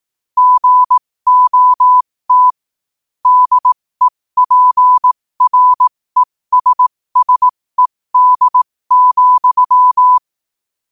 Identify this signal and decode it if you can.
Morse code, heard as a keyed tone, about 18 wpm, spaced as for 15 wpm, 1000 Hz. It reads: GOT DEPRESSED,
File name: qotdfiles / 15wpm